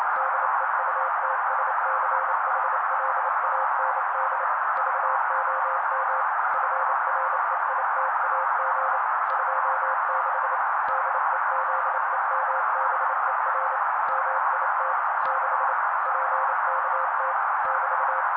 Morse code, heard as a keyed tone, about 28 wpm, 530 Hz. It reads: DEUTSCHLAND VOM FREITAG 26 DEZEMBER MIT 6 PKT <BT>
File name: daily / 27wpm noise